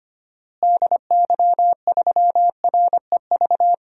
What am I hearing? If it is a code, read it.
Morse code, heard as a keyed tone, about 25 wpm, 700 Hz. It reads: DY3REV
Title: DXCC-100 Call Signs - 6 Characters - Repeated 3x - 25wpm